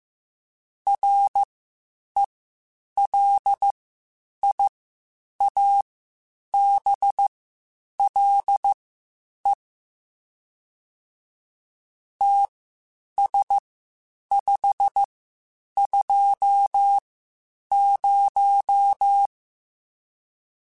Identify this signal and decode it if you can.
Morse code, heard as a keyed tone, about 15 wpm, spaced as for 8 wpm, 800 Hz. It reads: RELIABLE TS520
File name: (spaced) / QSOs Spaced-mp3s / WD0FBT 1 @13 sp